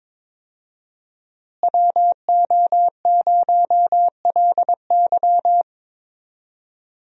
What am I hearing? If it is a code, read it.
Morse code, heard as a keyed tone, about 22 wpm, 700 Hz. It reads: WO0LY
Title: USA Call Signs - 5 Characters - Repeated 3x - 22wpm